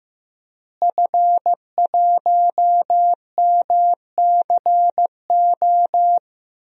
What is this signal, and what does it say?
Morse code, heard as a keyed tone, about 15 wpm, 700 Hz. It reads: F1MCO
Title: DXCC-100 Call Signs - 5 Characters - Repeated 2x - 15wpm